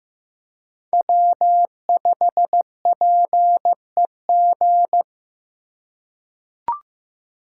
Morse code, heard as a keyed tone, about 15 wpm, 700 Hz. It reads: W5PEG
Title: USA Call Signs - 3x Character Spacing - 15wpm